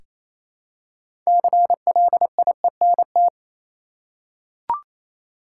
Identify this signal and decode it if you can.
Morse code, heard as a keyed tone, about 28 wpm, 700 Hz. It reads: CLIENT
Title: Six Letter Words - 28wpm